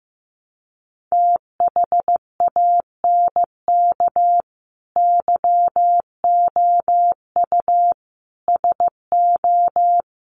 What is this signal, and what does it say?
Morse code, heard as a keyed tone, about 15 wpm, 700 Hz. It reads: THANK YOU SO
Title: Sets of 4 Words 15wpm